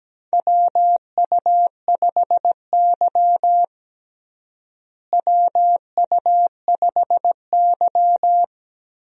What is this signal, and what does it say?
Morse code, heard as a keyed tone, about 17 wpm, 700 Hz. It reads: WU5Y WU5Y
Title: USA Call Signs - 4 Characters - Repeated 3x - 17wpm